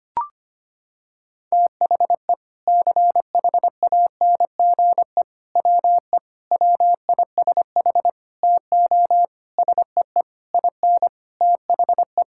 Morse code, heard as a keyed tone, about 25 wpm, 700 Hz. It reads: THE CHANGE WE WISH TO SEE IN THE
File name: Sets of 9 Words - 25wpm